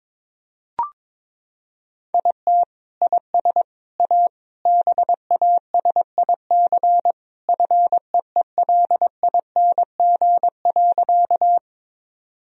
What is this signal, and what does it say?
Morse code, heard as a keyed tone, about 22 wpm, 700 Hz. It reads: IT IS A BASIC FEELING.
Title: Sentences from Top 1000 Words - Review Part 3 - 22wpm